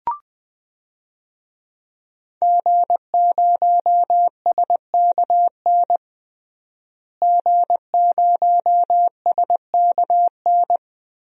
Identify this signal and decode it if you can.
Morse code, heard as a keyed tone, about 20 wpm, 700 Hz. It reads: G0SKN G0SKN
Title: DXCC-100 Call Signs - 5 Characters - Repeated 2x - 20wpm